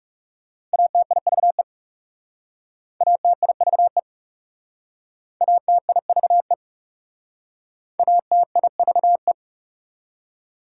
Morse code, heard as a keyed tone, about 38 wpm, 700 Hz. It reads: ATIVE ATIVE ATIVE ATIVE